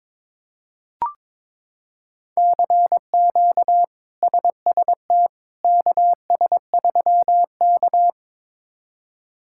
Morse code, heard as a keyed tone, about 22 wpm, 700 Hz. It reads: CQ SST KS3K
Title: K1USN SST QSO Practice - 22wpm